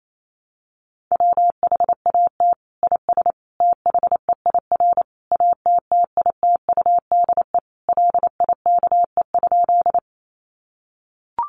Morse code, heard as a keyed tone, about 28 wpm, 700 Hz. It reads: WHAT IS THEIR ATTITUDE LIKE?
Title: Sentences from Top 1300 Words 28wpm